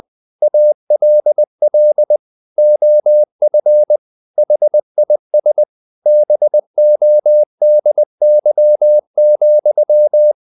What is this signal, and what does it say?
Morse code, heard as a keyed tone, about 20 wpm, 600 Hz. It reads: ALL OF HIS BODY,
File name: the old man and the sea part 2 20wpm